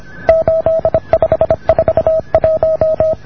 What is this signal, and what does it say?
Morse code, heard as a keyed tone, about 26 wpm, 620 Hz. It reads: 8541